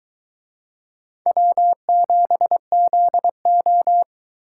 Morse code, heard as a keyed tone, about 20 wpm, 700 Hz. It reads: W7ZO
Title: Speed Racing - USA 1x2 Call Signs 17wpm